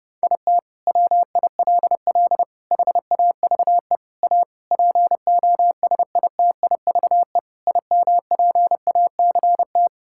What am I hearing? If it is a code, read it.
Morse code, heard as a keyed tone, about 30 wpm, 700 Hz. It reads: IT WILL HAVE A POSITIVE IMPACT
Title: Sentences from Top 1000 Words - Review Part 6 - 30wpm